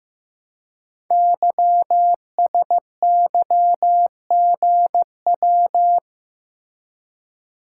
Morse code, heard as a keyed tone, about 15 wpm, 700 Hz. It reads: YSYGW
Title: Code Groups - 5 Letters - 15wpm